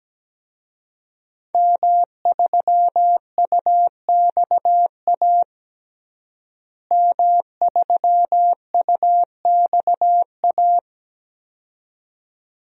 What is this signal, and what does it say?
Morse code, heard as a keyed tone, about 17 wpm, 700 Hz. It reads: M3UXA M3UXA
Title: DXCC-100 Call Signs - 5 Characters - Repeated 2x - 17wpm